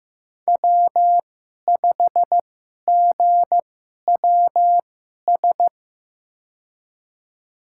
Morse code, heard as a keyed tone, about 15 wpm, spaced as for 11 wpm, 700 Hz. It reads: W5GWS
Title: USA Call Signs - 5 Characters - 2x Character Spacing - 15wpm